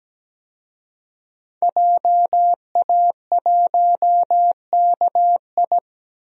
Morse code, heard as a keyed tone, about 17 wpm, 700 Hz. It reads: JA1KI
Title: DXCC-100 Call Signs - 5 Characters - 17wpm